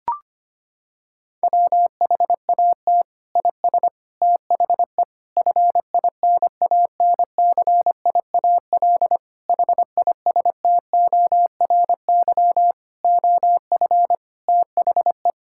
Morse code, heard as a keyed tone, about 25 wpm, 700 Hz. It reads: WHAT IS THE FINANCIAL HISTORY OF THE
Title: Sentences from Top 1000 Words - Review Part 5 - 25wpm